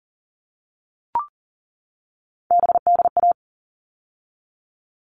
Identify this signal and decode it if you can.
Morse code, heard as a keyed tone, about 40 wpm, 700 Hz. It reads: BDA
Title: All Cut Numbers - 3-Digit 40wpm